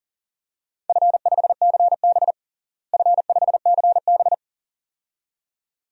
Morse code, heard as a keyed tone, about 40 wpm, 700 Hz. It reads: F5CB F5CB
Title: DXCC-100 Call Signs - 4 Characters - Repeated 2x - 40wpm